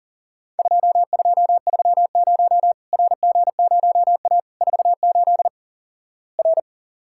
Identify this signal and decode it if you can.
Morse code, heard as a keyed tone, about 37 wpm, 700 Hz. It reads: 2230 RG0A 48 R
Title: Worked All Europe DX Contest - QTC Exchange - 40wpm